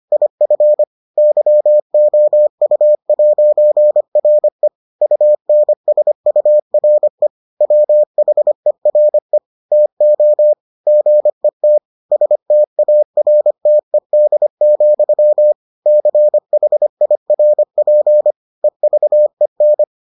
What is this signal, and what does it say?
Morse code, heard as a keyed tone, about 25 wpm, 600 Hz. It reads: IF YOU'RE UNSURE WHERE TO GET STARTED, CHIRP EVEN